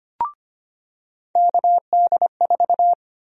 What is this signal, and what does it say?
Morse code, heard as a keyed tone, about 25 wpm, 700 Hz. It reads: KD4
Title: USA Call Sign Prefixes - 25wpm